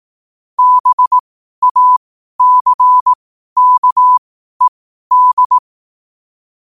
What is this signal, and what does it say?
Morse code, heard as a keyed tone, about 18 wpm, spaced as for 12 wpm, 1000 Hz. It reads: BACKED
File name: qotdfiles / 12wpm